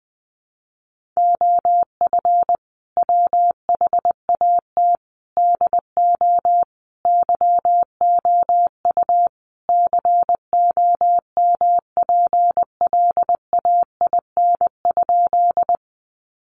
Morse code, heard as a keyed tone, about 20 wpm, 700 Hz. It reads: OF WHAT DO YOU COMPLAIN?
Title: Sentences from Top 1300 Words 20wpm